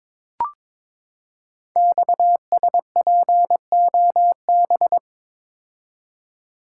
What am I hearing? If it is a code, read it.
Morse code, heard as a keyed tone, about 22 wpm, 700 Hz. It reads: XSPOB
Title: Code Groups - 5 Letters - 22wpm